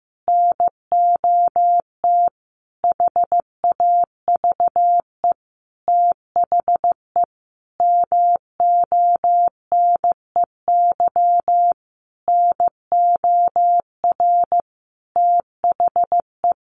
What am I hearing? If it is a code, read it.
Morse code, heard as a keyed tone, about 15 wpm, 700 Hz. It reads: NOT HAVE THE MONEY NOR THE
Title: Sentences from Top 1500 Words 15wpm